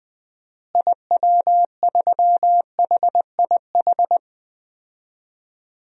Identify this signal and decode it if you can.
Morse code, heard as a keyed tone, about 21 wpm, 700 Hz. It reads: IW3HIH
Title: DXCC-100 Call Signs - 6 Characters - Repeated 2x - 20wpm